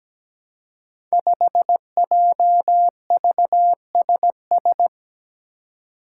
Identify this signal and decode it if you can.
Morse code, heard as a keyed tone, about 17 wpm, 700 Hz. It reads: 5JVSS
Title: Five Challenging Letter-Number Combinations 17wpm